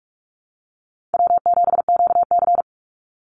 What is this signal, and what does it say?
Morse code, heard as a keyed tone, about 44 wpm, 700 Hz. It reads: W7QC
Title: Easy Call Signs Repeated 2x - 45wpm